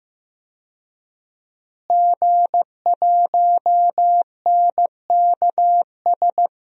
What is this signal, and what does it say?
Morse code, heard as a keyed tone, about 15 wpm, 700 Hz. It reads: G1NKS
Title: DXCC-100 Call Signs - 5 Characters - Repeated 3x - 15wpm